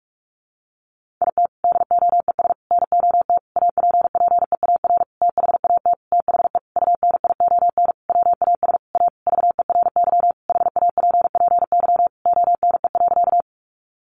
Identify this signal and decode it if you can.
Morse code, heard as a keyed tone, about 44 wpm, 700 Hz. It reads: IT DOES NOT APPEAR THAT THE UNION WAS A VERY HAPPY ONE.
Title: Sentences from Top 1400 Words 45wpm